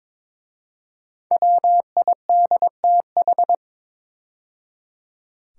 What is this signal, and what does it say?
Morse code, heard as a keyed tone, about 22 wpm, 700 Hz. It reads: WIDTH
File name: 5 Letter Words - First 3 Letters Given - 22wpm